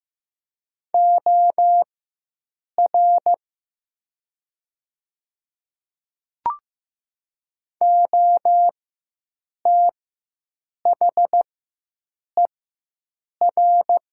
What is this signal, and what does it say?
Morse code, heard as a keyed tone, about 15 wpm, spaced as for 7 wpm, 700 Hz. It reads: OR OTHER
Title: 4x Character Spacing - Top 100 Words - 15wpm